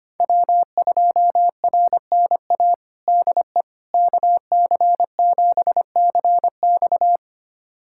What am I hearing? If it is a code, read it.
Morse code, heard as a keyed tone, about 25 wpm, 700 Hz. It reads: W2RNA DE KC7CX
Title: USA Callsign DE Callsign - 5 Characters - 25wpm